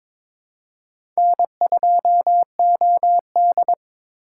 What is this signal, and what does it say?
Morse code, heard as a keyed tone, about 22 wpm, 700 Hz. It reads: N2OD
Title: USA Call Signs - 3x Character Spacing - 22wpm